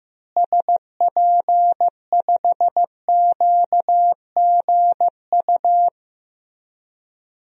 Morse code, heard as a keyed tone, about 16 wpm, 700 Hz. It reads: SP5QGU
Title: DXCC-100 Call Signs - 6 Characters - Repeated 3x - 15wpm